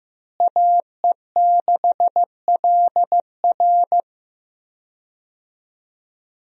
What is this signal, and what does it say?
Morse code, heard as a keyed tone, about 15 wpm, 700 Hz. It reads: AE6LR
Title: USA Call Signs - 5 Characters - Repeated 2x - 15wpm